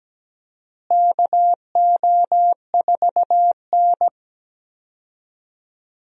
Morse code, heard as a keyed tone, about 17 wpm, 700 Hz. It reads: KO4N
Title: USA Call Signs - 4 Characters - Repeated 3x - 17wpm